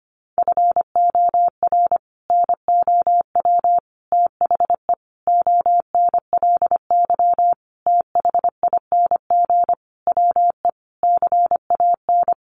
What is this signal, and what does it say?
Morse code, heard as a keyed tone, about 25 wpm, 700 Hz. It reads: FOR NOW THE ONLY THING WE CAN
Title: Sentences from Top 200 Words 25wpm